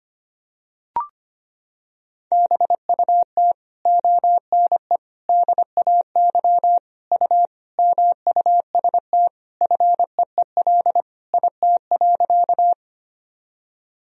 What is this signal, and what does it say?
Morse code, heard as a keyed tone, about 25 wpm, 700 Hz. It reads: BUT ONE DAY U MUST FEEL IT.
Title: Sentences from Top 400 Words and Abbreviations 25wpm